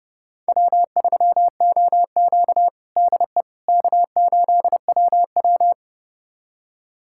Morse code, heard as a keyed tone, about 32 wpm, 700 Hz. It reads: W3OQ DE K8WW
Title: USA Callsign DE Callsign - 4 Characters - 30wpm